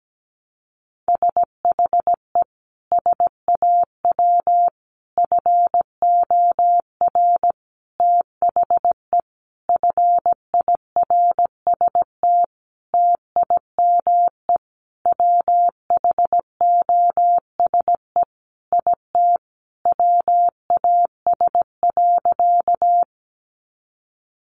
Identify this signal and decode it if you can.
Morse code, heard as a keyed tone, about 17 wpm, 700 Hz. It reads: SHE SAW FOR THE FIRST TIME WHOSE IT WAS.
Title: Sentences from Top 1200 Words 17wpm